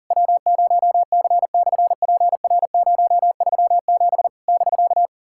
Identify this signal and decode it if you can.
Morse code, heard as a keyed tone, about 41 wpm, 700 Hz. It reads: W0C/PR037 <BK>
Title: SOTA QSO Practice 40wpm